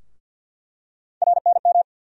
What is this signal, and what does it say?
Morse code, heard as a keyed tone, about 46 wpm, 700 Hz. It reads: RNM
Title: USA Call Sign Postfixes - 3 Letters 50wpm